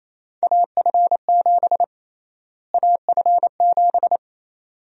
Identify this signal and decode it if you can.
Morse code, heard as a keyed tone, about 28 wpm, 700 Hz. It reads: AF7 AF7
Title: USA Call Sign Prefixes - Repeated 2x - 28wpm